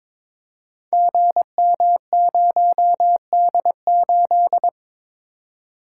Morse code, heard as a keyed tone, about 23 wpm, 700 Hz. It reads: GM0D8
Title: Code Groups - 5 Letter:Numbers - 22wpm